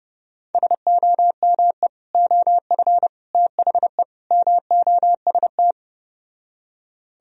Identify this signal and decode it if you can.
Morse code, heard as a keyed tone, about 32 wpm, 700 Hz. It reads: SOME OF THE MOST